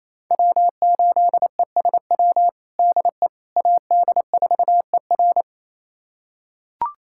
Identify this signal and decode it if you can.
Morse code, heard as a keyed tone, about 28 wpm, 700 Hz. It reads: W8ESW DE AD4ER
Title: USA Callsign DE Callsign - 5 Characters - 28wpm